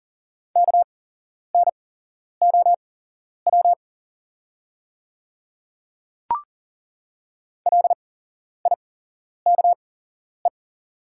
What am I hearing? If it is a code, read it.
Morse code, heard as a keyed tone, about 41 wpm, spaced as for 11 wpm, 700 Hz. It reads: KNOW LIKE
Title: 8x Character Spacing - Top 100 Words - 40wpm